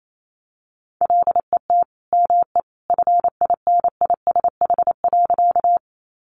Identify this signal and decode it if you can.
Morse code, heard as a keyed tone, about 28 wpm, 700 Hz. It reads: LET ME FINISH.
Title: Sentences from Top 1200 Words 28wpm